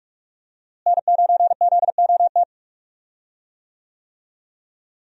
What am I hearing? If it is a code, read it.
Morse code, heard as a keyed tone, about 47 wpm, 700 Hz. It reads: N9ZOT
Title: USA Call Signs - 5 Characters - Repeated 2x - 45wpm